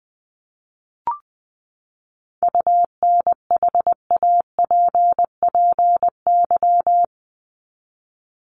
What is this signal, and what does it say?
Morse code, heard as a keyed tone, about 20 wpm, 700 Hz. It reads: UNHAPPY
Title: Top 1501-1750 Words 20wpm